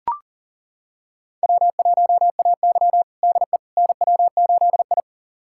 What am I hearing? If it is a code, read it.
Morse code, heard as a keyed tone, about 40 wpm, 700 Hz. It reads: W1AY DE NW8I
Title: USA Callsign DE Callsign - 4 Characters - 40wpm